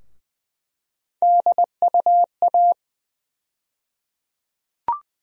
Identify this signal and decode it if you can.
Morse code, heard as a keyed tone, about 20 wpm, 700 Hz. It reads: DUA
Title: USA Call Sign Postfixes - 3 Letters 20wpm